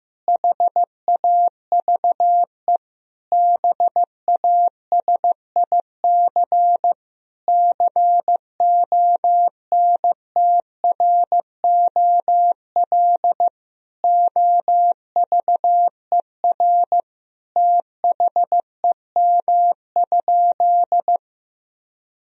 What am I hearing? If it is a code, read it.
Morse code, heard as a keyed tone, about 15 wpm, 700 Hz. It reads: HAVE BASIC CONTROL OVER THEM?
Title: Sentences from Top 1000 Words - Review Part 3 - 15wpm